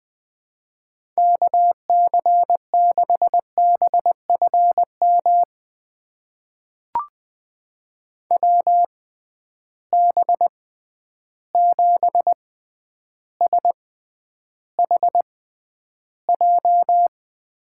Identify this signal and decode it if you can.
Morse code, heard as a keyed tone, about 21 wpm, spaced as for 10 wpm, 700 Hz. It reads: KC6BFM WB7SHJ
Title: USA Call Signs - 6 Characters - 6x Character Spacing - 20wpm